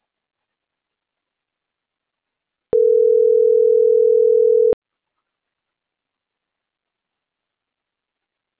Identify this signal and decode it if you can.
Ringback tone (North America), 440 + 480 Hz.